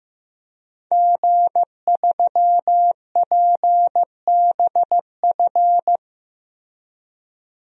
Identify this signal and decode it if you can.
Morse code, heard as a keyed tone, about 15 wpm, 700 Hz. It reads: G3PBF
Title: DXCC-100 Call Signs - 5 Characters - Repeated 3x - 15wpm